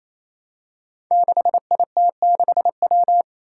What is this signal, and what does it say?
Morse code, heard as a keyed tone, about 28 wpm, 700 Hz. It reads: 6IT6W
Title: Five Challenging Letter-Number Combinations 28wpm